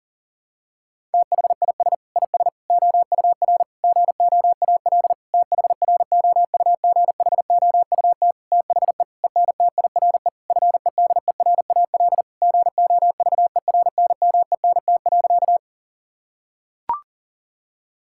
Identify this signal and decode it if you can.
Morse code, heard as a keyed tone, about 40 wpm, 700 Hz. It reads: THIS IS OUR GOAL THROUGHOUT THE ENTIRE FEDERAL GOVERNMENT.
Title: Sentences from Top 1000 Words - Review Part 5 - 40wpm